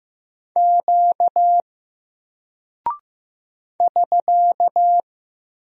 Morse code, heard as a keyed tone, about 15 wpm, 700 Hz. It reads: Q <SK>
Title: Single Character (Letter:Number:Punctuation:Prosign) 15wpm